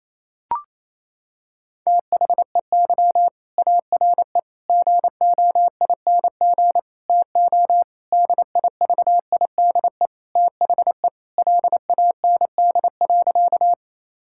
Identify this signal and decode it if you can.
Morse code, heard as a keyed tone, about 27 wpm, 700 Hz. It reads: THEY ARE GOING TO DIVIDE THE LAND.
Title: Sentences from Top 1600 Words 28wpm